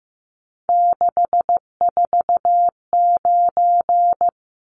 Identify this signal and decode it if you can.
Morse code, heard as a keyed tone, about 15 wpm, 700 Hz. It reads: 649
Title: Numbers - 3-Digit 15wpm